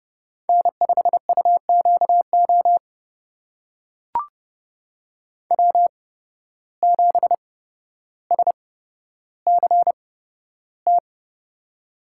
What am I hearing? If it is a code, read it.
Morse code, heard as a keyed tone, about 32 wpm, spaced as for 13 wpm, 700 Hz. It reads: N5UQO W7SCT
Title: USA Call Signs - 8x Character Spacing - 30wpm